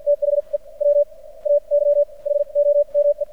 Morse code, heard as a keyed tone, about 53 wpm, spaced as for 35 wpm, 580 Hz. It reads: THE WAY HOME
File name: archived / CRSnet